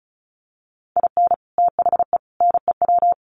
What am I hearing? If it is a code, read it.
Morse code, heard as a keyed tone, about 34 wpm, 700 Hz. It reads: IN THE NEW
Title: Sets of 4 Words 35wpm